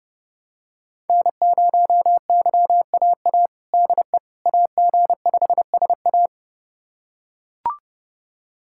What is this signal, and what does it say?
Morse code, heard as a keyed tone, about 30 wpm, 700 Hz. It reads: N0YAA DE AG5SA
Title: USA Callsign DE Callsign 30wpm